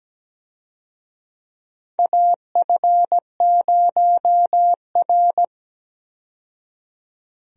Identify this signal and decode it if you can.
Morse code, heard as a keyed tone, about 17 wpm, 700 Hz. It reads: AF0R
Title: USA Call Signs - 4 Characters - 17wpm